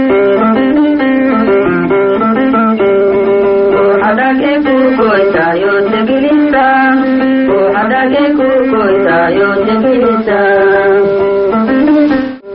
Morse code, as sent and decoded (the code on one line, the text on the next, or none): none